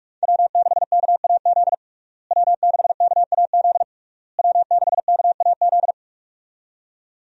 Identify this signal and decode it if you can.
Morse code, heard as a keyed tone, about 43 wpm, 700 Hz. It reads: W6KAZ W6KAZ W6KAZ
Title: USA Call Signs - 5 Characters - Repeated 3x - 45wpm